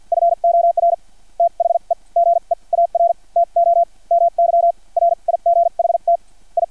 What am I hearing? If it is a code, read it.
Morse code, heard as a keyed tone, about 50 wpm, spaced as for 40 wpm, 680 Hz. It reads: FOR THE GEAR TO MY RIGHT I
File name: archived / CRSnet